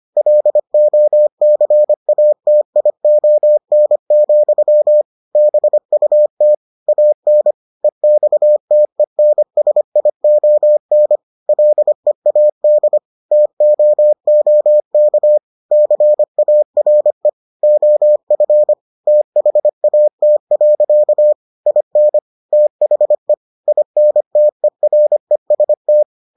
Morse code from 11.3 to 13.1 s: LEAD

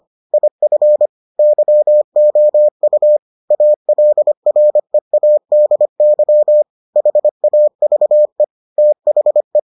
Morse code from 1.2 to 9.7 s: YOU ALREADY HAVE THE

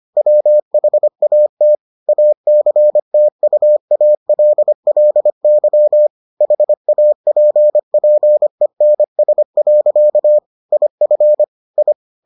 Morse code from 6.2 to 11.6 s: HAPPENS. IF

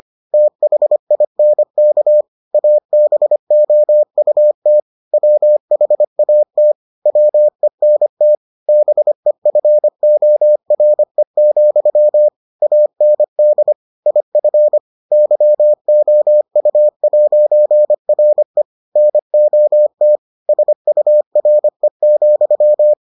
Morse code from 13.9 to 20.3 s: IF YOU'RE NOT

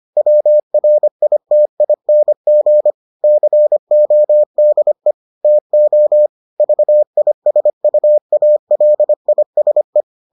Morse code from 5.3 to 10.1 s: TO VISUALISE

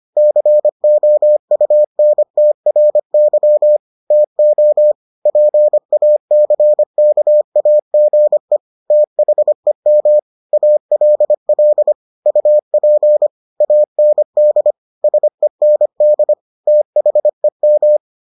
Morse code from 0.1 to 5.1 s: COUNTRY TO

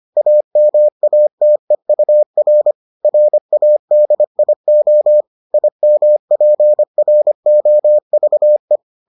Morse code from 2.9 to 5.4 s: RADIO

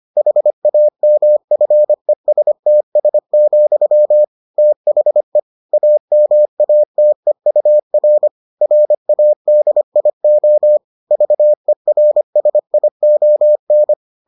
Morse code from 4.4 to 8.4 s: THE AMATEUR

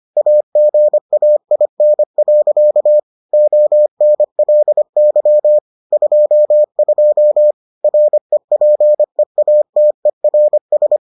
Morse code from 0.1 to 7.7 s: AGAIN. ONLY 22

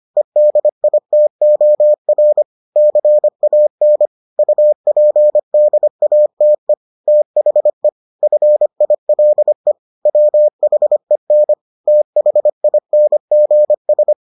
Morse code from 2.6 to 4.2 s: CAN

